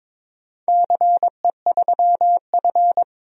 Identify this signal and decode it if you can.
Morse code, heard as a keyed tone, about 22 wpm, 700 Hz. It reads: CE3F
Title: DXCC-100 Call Signs - 4 Characters - 22wpm